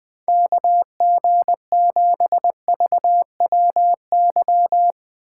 Morse code, heard as a keyed tone, about 20 wpm, 700 Hz. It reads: KG7VWY